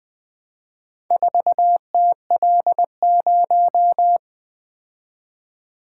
Morse code, heard as a keyed tone, about 20 wpm, 700 Hz. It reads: 4TL0
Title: Four Letter-Number Combinations - 20wpm